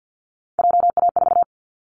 Morse code, heard as a keyed tone, about 50 wpm, 700 Hz. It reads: JA4
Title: Callsign Prefixes for Top 25 DXCC - 50wpm